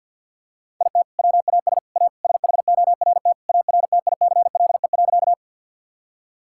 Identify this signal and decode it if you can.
Morse code, heard as a keyed tone, about 52 wpm, 700 Hz. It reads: IT WAS A SHORT ARTICLE.
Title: Sentences from Top 1000 Words - Review Part 5 - 50wpm